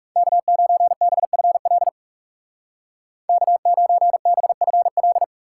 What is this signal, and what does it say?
Morse code, heard as a keyed tone, about 40 wpm, 700 Hz. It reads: K9BFL K9BFL